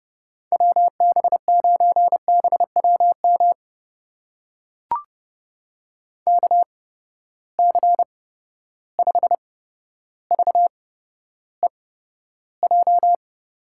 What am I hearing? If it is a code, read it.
Morse code, heard as a keyed tone, about 30 wpm, spaced as for 13 wpm, 700 Hz. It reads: WB9BWM KC5VEJ
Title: USA Call Signs - 6 Characters - 8x Character Spacing - 30wpm